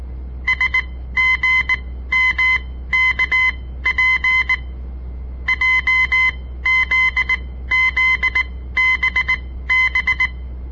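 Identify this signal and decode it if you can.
Morse code, heard as a keyed tone, about 18 wpm, spaced as for 14 wpm, 2050 Hz. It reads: SGMKP JZZBB
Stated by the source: On-air Morse code practice broadcast by a southern Montana amateur radio association; all code, no read back.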